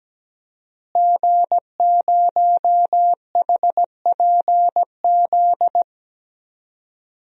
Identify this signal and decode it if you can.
Morse code, heard as a keyed tone, about 17 wpm, 700 Hz. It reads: G0HPZ
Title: DXCC-100 Call Signs - 5 Characters - Repeated 2x - 17wpm